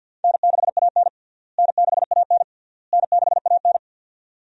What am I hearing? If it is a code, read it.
Morse code, heard as a keyed tone, about 51 wpm, 700 Hz. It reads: N6AN N6AN N6AN
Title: USA Call Signs - 4 Characters - Repeated 3x - 50wpm